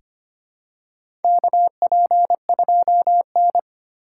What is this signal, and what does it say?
Morse code, heard as a keyed tone, about 25 wpm, 700 Hz. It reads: KP2N
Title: K1USN SST QSO Practice - 25wpm